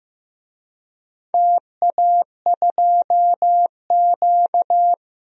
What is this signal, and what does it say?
Morse code, heard as a keyed tone, about 15 wpm, 700 Hz. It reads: TA2Q
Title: DXCC-100 Call Signs - 4 Characters - 15wpm